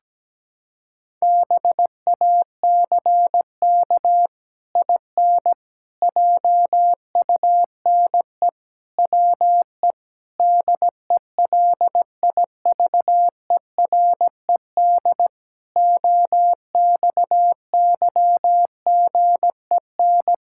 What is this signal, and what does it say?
Morse code, heard as a keyed tone, about 17 wpm, 700 Hz. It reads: BACK IN JUNE WE DELIVERED OXYGEN
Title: Pangram Challenge v2 - 17wpm